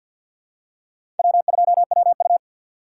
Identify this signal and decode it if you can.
Morse code, heard as a keyed tone, about 48 wpm, 700 Hz. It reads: W2WU
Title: USA Call Signs - 4 Characters - 2x Character Spacing - 50wpm